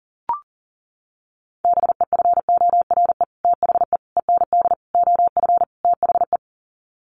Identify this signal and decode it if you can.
Morse code, heard as a keyed tone, about 40 wpm, 700 Hz. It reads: BEFORE THE END OF THE
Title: Sets of 5 Words Encore 40wpm